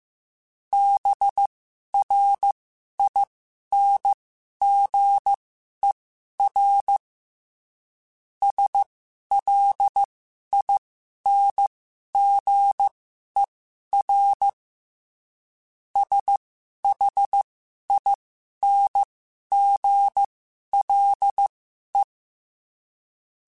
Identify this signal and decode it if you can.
Morse code, heard as a keyed tone, about 15 wpm, spaced as for 11 wpm, 800 Hz. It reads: BRINGER SLINGER SHINGLE